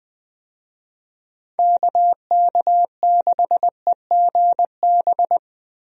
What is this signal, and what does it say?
Morse code, heard as a keyed tone, about 20 wpm, 700 Hz. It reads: KK6EGB
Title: USA Call Signs - 6 Characters - Repeated 3x - 20wpm